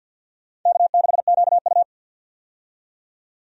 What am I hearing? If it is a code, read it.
Morse code, heard as a keyed tone, about 51 wpm, 700 Hz. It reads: K6QU
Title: USA Call Signs - 4 Characters - 50wpm